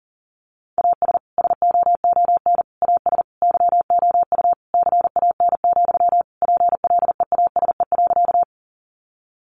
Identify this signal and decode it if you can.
Morse code, heard as a keyed tone, about 40 wpm, 700 Hz. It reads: AS SOON AS YOU CAN, PLEASE.
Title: Sentences from Top 1500 Words 40wpm